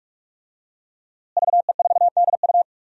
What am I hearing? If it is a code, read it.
Morse code, heard as a keyed tone, about 46 wpm, 700 Hz. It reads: VE4DU
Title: DXCC-100 Call Signs - 5 Characters - Repeated 2x - 45wpm